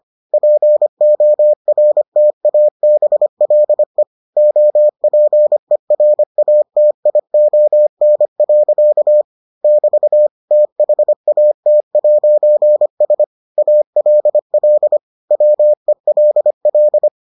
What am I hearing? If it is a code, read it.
Morse code, heard as a keyed tone, about 25 wpm, 600 Hz. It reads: PORTABLE OPERATION. <BT> THAT'S ALL WELL